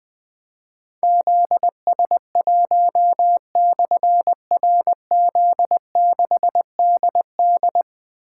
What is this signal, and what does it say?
Morse code, heard as a keyed tone, about 20 wpm, 700 Hz. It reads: ZS1/RZ6DD
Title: Difficult Call Signs 20wpm